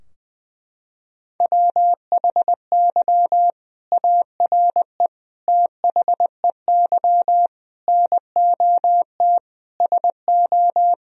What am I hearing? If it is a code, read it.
Morse code, heard as a keyed tone, about 21 wpm, 700 Hz. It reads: WHY ARE THEY NOT SO
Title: Sentences from Top 1000 Words - Review Part 4 - 20wpm